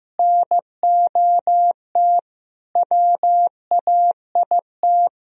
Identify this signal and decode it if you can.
Morse code, heard as a keyed tone, about 15 wpm, 700 Hz. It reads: NOT WAIT
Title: Sentences from Top 1000 Words - Review Part 3 - 15wpm